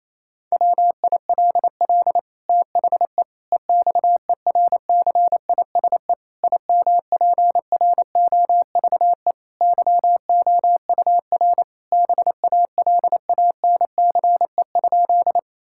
Morse code, heard as a keyed tone, about 28 wpm, 700 Hz. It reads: WILL THE EXERCISE IMPROVE YOUR BALANCE?
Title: Sentences from Top 1000 Words - Review Part 6 - 28wpm